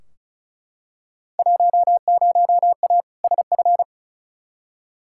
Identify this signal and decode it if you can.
Morse code, heard as a keyed tone, about 36 wpm, 700 Hz. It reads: 10A SF
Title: Field Day Exchanges Encore - 35wpm